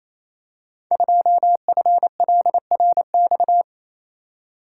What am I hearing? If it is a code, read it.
Morse code, heard as a keyed tone, about 28 wpm, 700 Hz. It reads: 2FLRX
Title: Code Groups - 5 Letter:Numbers - 28wpm